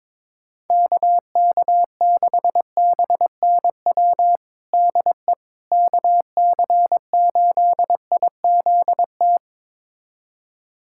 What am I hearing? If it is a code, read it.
Morse code, heard as a keyed tone, about 22 wpm, 700 Hz. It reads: KK6BNW DE KC8IZT
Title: USA Callsign DE Callsign 22wpm